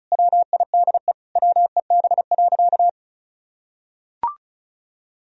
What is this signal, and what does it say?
Morse code, heard as a keyed tone, about 36 wpm, 700 Hz. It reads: WIDE WEB.
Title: Sentences from Top 1000 Words - Review Part 4 - 35wpm